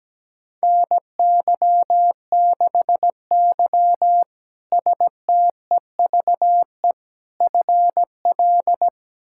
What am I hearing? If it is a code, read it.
Morse code, heard as a keyed tone, about 17 wpm, 700 Hz. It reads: NY6Y STEVE FL
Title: K1USN SST QSO Practice - 17wpm